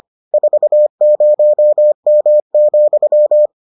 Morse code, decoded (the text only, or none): 40M,